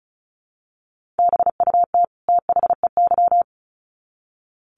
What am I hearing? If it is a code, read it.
Morse code, heard as a keyed tone, about 34 wpm, 700 Hz. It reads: BUT THEY
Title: Sets of 2 Words 35wpm